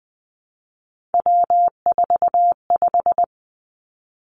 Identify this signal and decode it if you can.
Morse code, heard as a keyed tone, about 20 wpm, 700 Hz. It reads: W45
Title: Three Challenging Letter-Number Combinations 20wpm